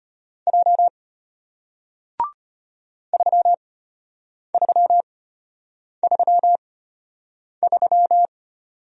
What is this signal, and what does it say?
Morse code, heard as a keyed tone, about 33 wpm, spaced as for 8 wpm, 700 Hz. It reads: J 3333